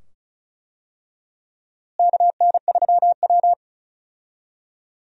Morse code, heard as a keyed tone, about 34 wpm, 700 Hz. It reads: KN3W
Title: USA Call Signs - 4 Characters - Repeated 2x - 35wpm